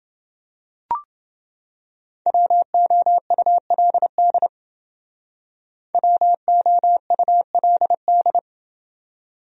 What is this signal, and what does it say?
Morse code, heard as a keyed tone, about 28 wpm, 700 Hz. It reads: WOULD WOULD